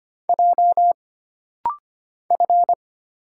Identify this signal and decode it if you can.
Morse code, heard as a keyed tone, about 25 wpm, 700 Hz. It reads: J F
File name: Single Character (Letter:Number:Punctuation:Prosign) - Rapid-Fire - 25wpm